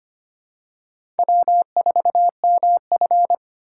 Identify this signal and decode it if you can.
Morse code, heard as a keyed tone, about 25 wpm, 700 Hz. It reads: W4MF
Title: USA Call Signs - 4 Characters - 3x Character Spacing - 25wpm